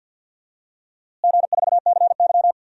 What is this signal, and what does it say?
Morse code, heard as a keyed tone, about 51 wpm, 700 Hz. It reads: G4CY
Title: DXCC-100 Call Signs - 4 Characters - Repeated 2x - 50wpm